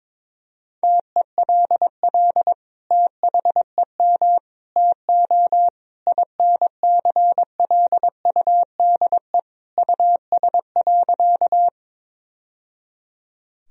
Morse code, heard as a keyed tone, about 22 wpm, 700 Hz. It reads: TELL THEM TO INCLUDE US.
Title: Sentences from Top 1000 Words - Review Part 3 - 22wpm